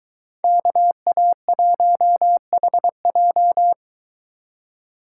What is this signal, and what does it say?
Morse code, heard as a keyed tone, about 23 wpm, 700 Hz. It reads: KA1HJ